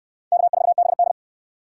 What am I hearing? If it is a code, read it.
Morse code, heard as a keyed tone, about 63 wpm, 700 Hz. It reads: K4BD